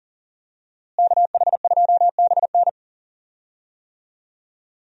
Morse code, heard as a keyed tone, about 38 wpm, 700 Hz. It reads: KH2BN
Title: USA Call Signs - 5 Characters - Repeated 2x - 40wpm